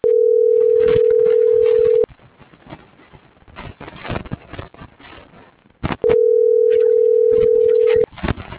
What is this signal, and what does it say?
Ringback tone (North America), 440 + 480 Hz.